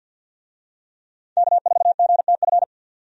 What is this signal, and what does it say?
Morse code, heard as a keyed tone, about 50 wpm, 700 Hz. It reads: K4GTF
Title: USA Call Signs - 5 Characters - 50wpm